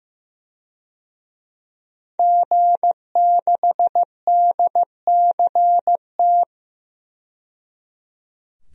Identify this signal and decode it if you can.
Morse code, heard as a keyed tone, about 16 wpm, 700 Hz. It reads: G6DCT
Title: DXCC-100 Call Signs - 5 Characters - 15wpm